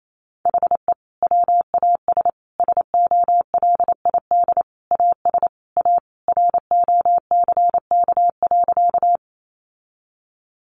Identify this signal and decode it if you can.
Morse code, heard as a keyed tone, about 28 wpm, 700 Hz. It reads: HE WAS SOLID AS A ROCK.
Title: Sentences from Top 1500 Words 28wpm